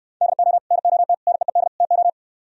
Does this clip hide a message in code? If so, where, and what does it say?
Yes: Morse code from 0.2 to 2.1 s, heard as a keyed tone, about 67 wpm, 700 Hz. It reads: DO NOT NEED TO